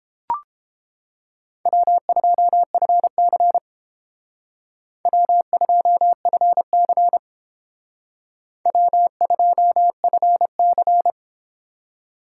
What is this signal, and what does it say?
Morse code, heard as a keyed tone, about 30 wpm, 700 Hz. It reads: W2FC W2FC W2FC